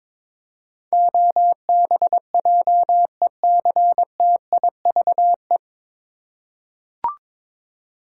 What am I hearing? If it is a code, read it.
Morse code, heard as a keyed tone, about 22 wpm, 700 Hz. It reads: OBJECTIVE
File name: 9 Letter Words - First 3 Letters Given - 22wpm